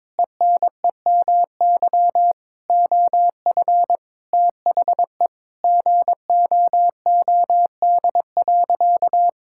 Morse code, heard as a keyed tone, about 22 wpm, 700 Hz. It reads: ENEMY OF THE GOOD.
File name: Sentences with 9-Word Phrases - 22wpm